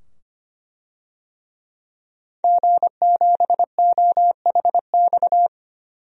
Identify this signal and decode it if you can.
Morse code, heard as a keyed tone, about 25 wpm, 700 Hz. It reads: G7OHX
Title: DXCC-100 Call Signs - 5 Characters - 25wpm